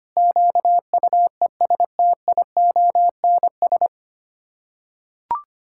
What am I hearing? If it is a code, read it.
Morse code, heard as a keyed tone, about 25 wpm, 700 Hz. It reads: QUESTIONS
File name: Sets of 9 Words - 25wpm